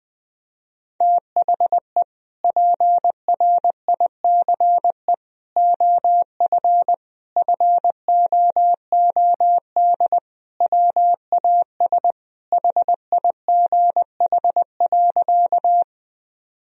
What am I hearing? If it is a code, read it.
Morse code, heard as a keyed tone, about 20 wpm, 700 Hz. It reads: THE PRICE OF FOOD WAS HIGH.
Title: Sentences from Top 1000 Words - Review Part 4 - 20wpm